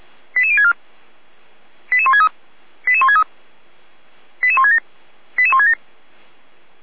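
Five-tone selective call (ZVEI), heard as quick runs of five tones, tones of about 70 ms. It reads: 88064 88163 88163 88167 88167
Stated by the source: ZVEI five-tone selective call received on SDR.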